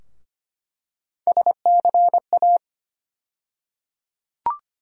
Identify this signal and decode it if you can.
Morse code, heard as a keyed tone, about 25 wpm, 700 Hz. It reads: SCA
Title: USA Call Sign Postfixes - 3 Letters 25wpm